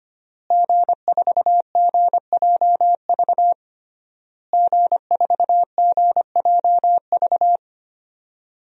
Morse code, heard as a keyed tone, about 25 wpm, 700 Hz. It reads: G4GJV G4GJV
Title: DXCC-100 Call Signs - 5 Characters - Repeated 3x - 25wpm